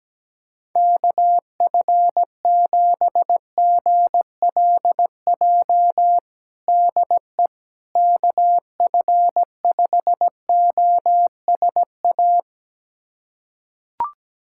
Morse code, heard as a keyed tone, about 17 wpm, 700 Hz. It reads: KF7GLJ DE KF5OSA
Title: USA Callsign DE Callsign - 6 Characters - 17wpm